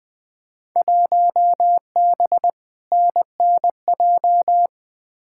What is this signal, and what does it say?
Morse code, heard as a keyed tone, about 21 wpm, 700 Hz. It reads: 1B NNJ
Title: Field Day Exchanges - 20wpm